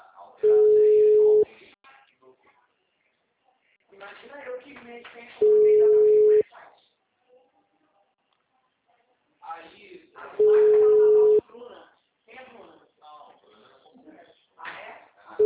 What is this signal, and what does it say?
Ringback tone (Europe, 425 Hz), with speech.
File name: out- -2009- - - .89300